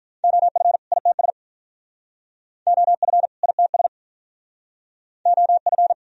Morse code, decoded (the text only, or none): OF ITS OF ITS OF